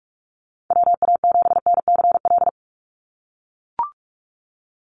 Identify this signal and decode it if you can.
Morse code, heard as a keyed tone, about 44 wpm, 700 Hz. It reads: WA7NCL
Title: NA SOTA Activator Callsigns 45wpm